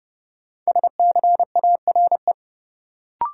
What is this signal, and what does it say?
Morse code, heard as a keyed tone, about 30 wpm, 700 Hz. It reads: SCARE
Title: Sentences from Top 1000 Words - Review Part 5 - New Word Drill - 30wpm